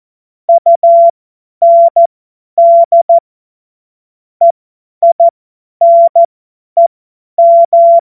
Morse code, heard as a keyed tone, about 14 wpm, spaced as for 10 wpm, 680 Hz. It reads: UND EINEM
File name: Morsekurs / Witze / Witz 31. Mai 2022 50 F70